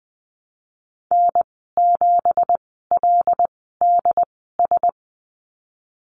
Morse code, heard as a keyed tone, about 20 wpm, spaced as for 14 wpm, 700 Hz. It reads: N7LDS
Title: 193 Lesson 31g - Letter v - Call Signs 20wpm